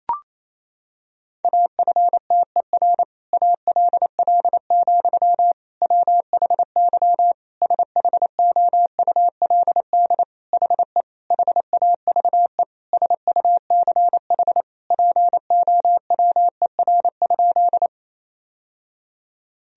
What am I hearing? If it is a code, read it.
Morse code, heard as a keyed tone, about 27 wpm, 700 Hz. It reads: AFTER ALL, WHY SHOULD HE HAVE SUCH POWER?
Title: Sentences from Top 1000 Words - Review Part 2 - 28wpm